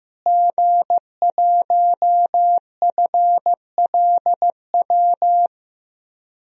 Morse code, heard as a keyed tone, about 15 wpm, 700 Hz. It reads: G1FLW
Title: DXCC-100 Call Signs - 5 Characters - Repeated 3x - 15wpm